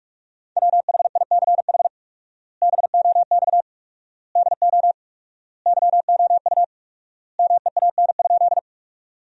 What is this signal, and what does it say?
Morse code, heard as a keyed tone, about 41 wpm, 690 Hz. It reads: WHICH BOX DO YOU MEAN?
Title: 4x Word-Spacing - Sentences from Top 500 Words - 45wpm